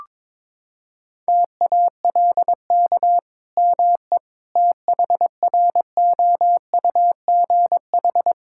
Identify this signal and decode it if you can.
Morse code, heard as a keyed tone, about 22 wpm, 700 Hz. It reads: TALK ME THROUGH
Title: Sentences from Top 1000 Words - Review Part 5 - 22wpm